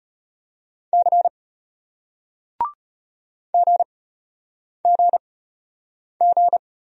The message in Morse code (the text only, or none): C GGG